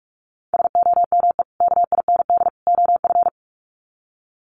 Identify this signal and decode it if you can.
Morse code, heard as a keyed tone, about 44 wpm, 700 Hz. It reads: SOME KIND OF
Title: Sets of 3 Words 45wpm